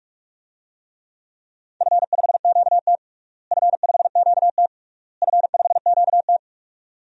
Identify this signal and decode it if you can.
Morse code, heard as a keyed tone, about 46 wpm, 690 Hz. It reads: F5QT F5QT F5QT
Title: DXCC-100 Call Signs - 4 Characters - Repeated 3x - 45wpm